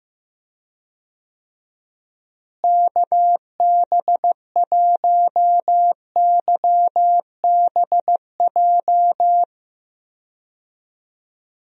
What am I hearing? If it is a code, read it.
Morse code, heard as a keyed tone, about 15 wpm, 700 Hz. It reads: KB1YBJ
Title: USA Call Signs - 6 Characters - Repeated 2x - 15wpm